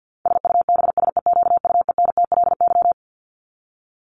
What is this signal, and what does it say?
Morse code, heard as a keyed tone, about 50 wpm, 700 Hz. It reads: SUBSEQUENTLY
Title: LY Suffix - Top 100 Words with Base Words 50wpm